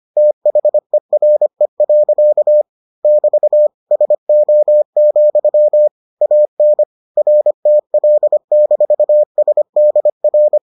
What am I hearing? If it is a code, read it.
Morse code, heard as a keyed tone, about 25 wpm, 600 Hz. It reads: THERE. <BT> SO, AN RTL-SDR